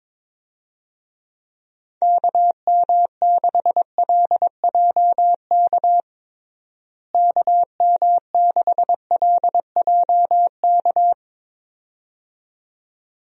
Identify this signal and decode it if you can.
Morse code, heard as a keyed tone, about 22 wpm, 700 Hz. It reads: KM6LJK KM6LJK
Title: USA Call Signs - 6 Characters - Repeated 2x - 22wpm